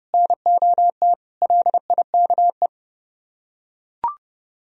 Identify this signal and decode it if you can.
Morse code, heard as a keyed tone, about 29 wpm, 700 Hz. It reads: NOT LIKE